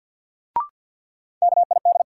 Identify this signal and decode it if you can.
Morse code, heard as a keyed tone, about 52 wpm, 700 Hz. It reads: KID
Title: Sentences from Top 1600 Words - New Word Drill - 50wpm